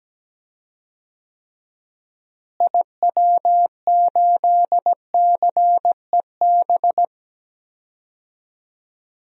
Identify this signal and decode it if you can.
Morse code, heard as a keyed tone, about 17 wpm, 700 Hz. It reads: IW8CEB
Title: DXCC-100 Call Signs - 6 Characters - Repeated 2x - 17wpm